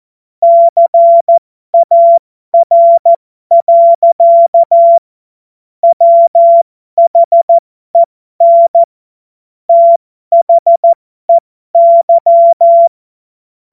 Morse code, heard as a keyed tone, about 14 wpm, spaced as for 12 wpm, 680 Hz. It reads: CAR. WHEN THEY
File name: Morsekurs / Witze